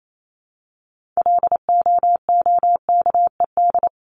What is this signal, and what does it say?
Morse code, heard as a keyed tone, about 28 wpm, 700 Hz. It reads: LOOKED
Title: ED Suffix - Top 100 Words with Base Words 28wpm